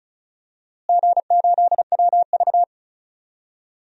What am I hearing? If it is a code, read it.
Morse code, heard as a keyed tone, about 34 wpm, 700 Hz. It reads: G8WV
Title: DXCC-100 Call Signs - 4 Characters - Repeated 3x - 35wpm